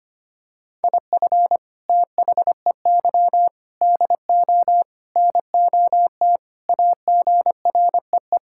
Morse code, heard as a keyed tone, about 25 wpm, 700 Hz. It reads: IF THEY DO NOT AGREE
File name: practice / 25wpm / Sentences with 4-Word Phrases - 25wpm